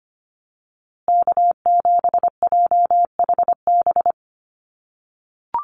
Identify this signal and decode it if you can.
Morse code, heard as a keyed tone, about 25 wpm, 700 Hz. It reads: K7JHB
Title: USA Call Signs Repeated 2x 25wpm